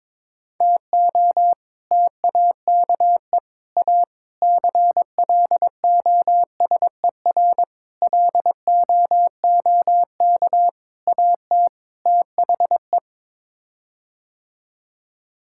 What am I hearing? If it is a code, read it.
Morse code, heard as a keyed tone, about 21 wpm, 700 Hz. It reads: TO TAKE A CLOSER LOOK AT THE
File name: practice / 22wpm / Sets of 7 Words - 22wpm